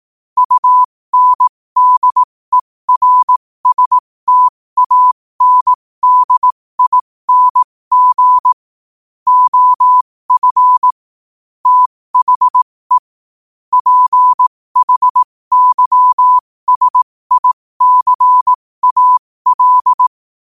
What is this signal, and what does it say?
Morse code, heard as a keyed tone, about 18 wpm, spaced as for 15 wpm, 1000 Hz. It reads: UNDERSTANDING OF THE PHYSICAL
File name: qotdfiles / 15wpm